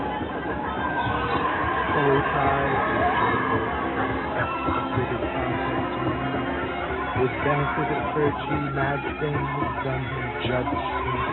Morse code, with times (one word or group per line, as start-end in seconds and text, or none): none